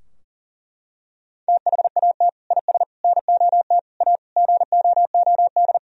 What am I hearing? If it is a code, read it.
Morse code, heard as a keyed tone, about 37 wpm, 700 Hz. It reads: THAT IS NOT A GOOD
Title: Sentences from Top 1000 Words - Review Part 3 - 40wpm